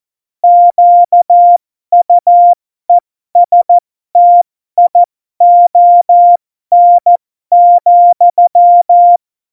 Morse code, heard as a keyed tone, about 14 wpm, spaced as for 12 wpm, 700 Hz. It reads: QUESTION,
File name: Morsekurs / Witze